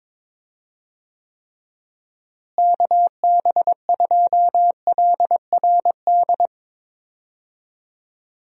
Morse code, heard as a keyed tone, about 21 wpm, 700 Hz. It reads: KB2LRD
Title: USA Call Signs - 6 Characters - Repeated 3x - 22wpm